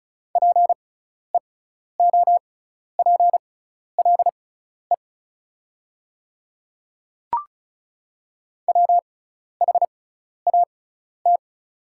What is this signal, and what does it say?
Morse code, heard as a keyed tone, about 34 wpm, spaced as for 12 wpm, 700 Hz. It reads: PEOPLE WHAT
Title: 6x Character Spacing - Top 100 Words - 35wpm